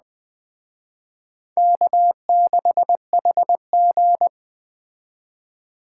Morse code, heard as a keyed tone, about 21 wpm, 700 Hz. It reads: K6HG